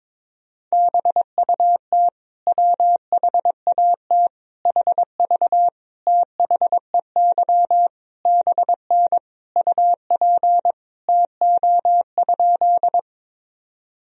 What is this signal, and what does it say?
Morse code, heard as a keyed tone, about 22 wpm, 700 Hz. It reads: BUT WHAT HV THEY BN UP TO?
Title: Sentences from Top 100 Words and Abbreviations 22wpm